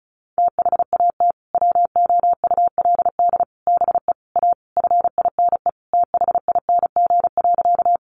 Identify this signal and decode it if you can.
Morse code, heard as a keyed tone, about 34 wpm, 700 Hz. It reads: THAT WOULD BE A FINE THING.
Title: Sentences from Top 500 Words 35wpm